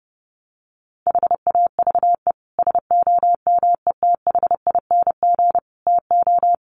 Morse code, heard as a keyed tone, about 28 wpm, 700 Hz. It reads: HAVE SOMETHING TO
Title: Sets of 5 Words Encore 30wpm